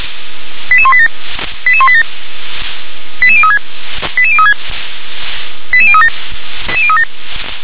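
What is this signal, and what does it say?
Five-tone selective call (ZVEI), heard as quick runs of five tones, tones of about 70 ms. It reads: 88187 88187 88036 88036 88037
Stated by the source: ZVEI five-tone call received by SDR.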